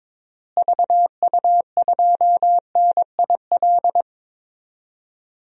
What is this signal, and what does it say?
Morse code, heard as a keyed tone, about 22 wpm, 700 Hz. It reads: VU2NIL
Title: DXCC-100 Call Signs - 6 Characters - Repeated 2x - 22wpm